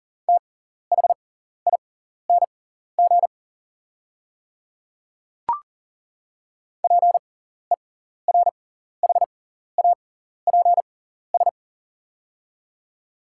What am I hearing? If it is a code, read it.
Morse code, heard as a keyed tone, about 40 wpm, spaced as for 14 wpm, 690 Hz. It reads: THING PERHAPS
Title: 6x Character Spacing - Top 500 Words - 40wpm